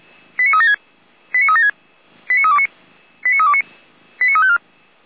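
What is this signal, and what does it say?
Five-tone selective call (ZVEI), heard as quick runs of five tones, tones of about 70 ms. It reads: 89376 89376 89329 89329 89354
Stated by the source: Five-tone ZVEI selective call received by SDR.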